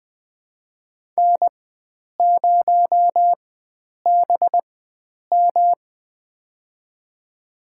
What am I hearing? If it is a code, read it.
Morse code, heard as a keyed tone, about 20 wpm, spaced as for 9 wpm, 700 Hz. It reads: N0BM
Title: USA Call Signs - 4 Characters - 4x Character Spacing - 20wpm